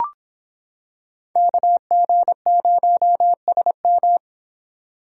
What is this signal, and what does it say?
Morse code, heard as a keyed tone, about 26 wpm, 700 Hz. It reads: KG0SM